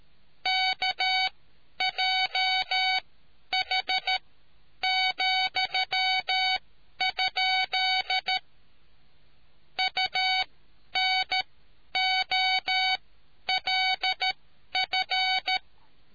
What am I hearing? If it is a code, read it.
Morse code, heard as a keyed tone, about 13 wpm, spaced as for 10 wpm, 2250 Hz, with noise. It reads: KJH,? UNOLF